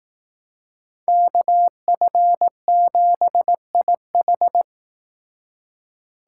Morse code, heard as a keyed tone, about 18 wpm, 700 Hz. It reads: KF7IH
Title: Speed Racing - USA 2x2 Call Signs 15wpm